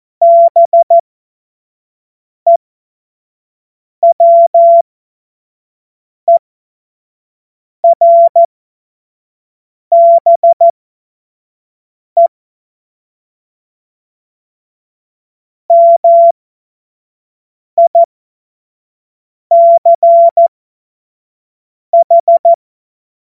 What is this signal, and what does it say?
Morse code, heard as a keyed tone, about 14 wpm, spaced as for 5 wpm, 680 Hz. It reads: BEWERBE MICH
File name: Morsekurs / Witze / Witz 10. Januar 2023 25 F70